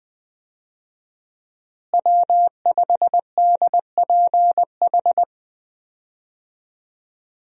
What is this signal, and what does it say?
Morse code, heard as a keyed tone, about 20 wpm, 700 Hz. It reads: W5DPH
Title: USA Call Signs - 5 Characters - Repeated 2x - 20wpm